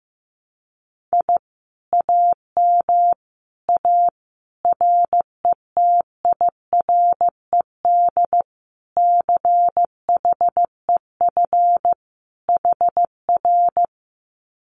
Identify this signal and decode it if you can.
Morse code, heard as a keyed tone, about 15 wpm, 700 Hz. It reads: I AM A RETIRED CHEF HR
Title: Common QSO Phrases Encore 15wpm